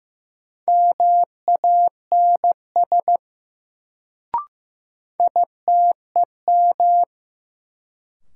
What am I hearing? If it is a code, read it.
Morse code, heard as a keyed tone, about 15 wpm, 700 Hz. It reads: MANS ITEM
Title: Sentences from Top 1000 Words - Review Part 6 - New Word Drill - 15wpm